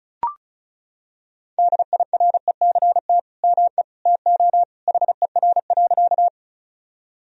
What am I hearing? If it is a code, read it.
Morse code, heard as a keyed tone, about 36 wpm, 700 Hz. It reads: DIRECT ME TO HER.
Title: Sentences from Top 1000 Words - Review Part 2 - 35wpm